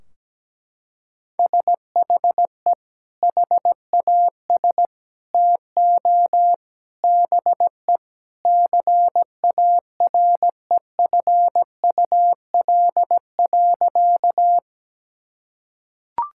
Morse code, heard as a keyed tone, about 17 wpm, 700 Hz. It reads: SHE HAS TO BE CAREFUL.
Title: Sentences from Top 1600 Words 17wpm